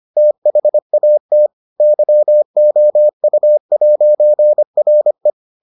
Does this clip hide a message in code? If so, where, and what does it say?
Yes: Morse code from 0.2 to 5.3 s, heard as a keyed tone, about 25 wpm, 600 Hz. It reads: THAT YOU'RE